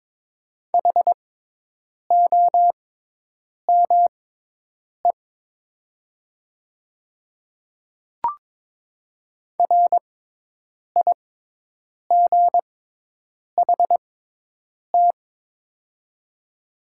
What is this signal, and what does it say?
Morse code, heard as a keyed tone, about 22 wpm, spaced as for 8 wpm, 700 Hz. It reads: HOME RIGHT
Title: 6x Character Spacing - Top 200 Words - 22wpm